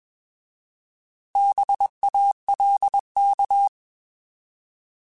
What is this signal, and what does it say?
Morse code, heard as a keyed tone, about 21 wpm, 800 Hz. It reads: BALK
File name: Bs @20